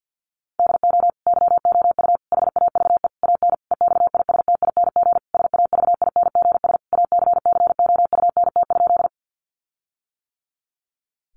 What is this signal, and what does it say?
Morse code, heard as a keyed tone, about 50 wpm, 700 Hz. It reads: DO YOU HAVE AN EXISTING SAVINGS ACCOUNT?
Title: Sentences from Top 1500 Words 50wpm